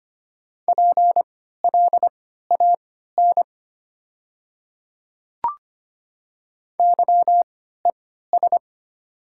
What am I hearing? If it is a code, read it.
Morse code, heard as a keyed tone, about 25 wpm, spaced as for 14 wpm, 690 Hz. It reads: PLAN YES
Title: 3x Character Spacing - Top 500 Words - 25wpm